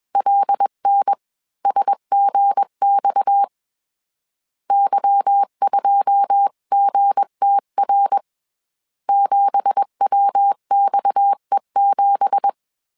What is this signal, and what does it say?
Morse code, heard as a keyed tone, about 21 wpm, 800 Hz. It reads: LNSGX Y2GTR 7WXE7
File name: CW / DET / 1 9 1500C S C 20wpm 2310